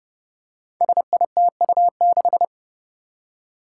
Morse code, heard as a keyed tone, about 30 wpm, 700 Hz. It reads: SITU6
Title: Five Challenging Letter-Number Combinations 30wpm